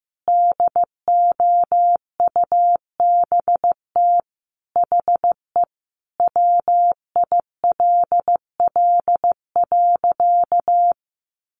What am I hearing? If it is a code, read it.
Morse code, heard as a keyed tone, about 15 wpm, 700 Hz. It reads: DOUBT HE WILL.
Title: Sentences from Top 1500 Words 15wpm